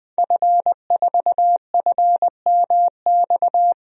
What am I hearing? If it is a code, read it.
Morse code, heard as a keyed tone, about 21 wpm, 700 Hz. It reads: F4FMX
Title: DXCC-100 Call Signs - 5 Characters - 20wpm